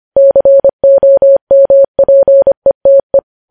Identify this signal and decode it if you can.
Morse code, heard as a keyed tone, about 25 wpm, 550 Hz. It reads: COMPETE